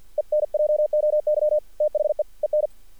Morse code, heard as a keyed tone, about 50 wpm, 600 Hz. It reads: ENJOY THE EN
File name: archived / CRSnet